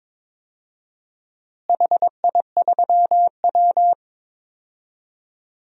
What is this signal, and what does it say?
Morse code, heard as a keyed tone, about 22 wpm, 700 Hz. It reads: HI3W
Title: DXCC-100 Call Signs - 4 Characters - Repeated 2x - 22wpm